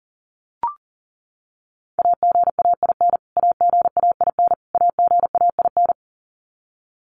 Morse code, heard as a keyed tone, about 37 wpm, 700 Hz. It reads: AGAIN AGAIN AGAIN
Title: Top 200 Words Repeated 40wpm